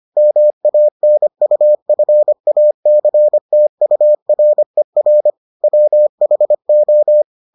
Morse code, heard as a keyed tone, about 25 wpm, 600 Hz. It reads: MANUFACTURER WHO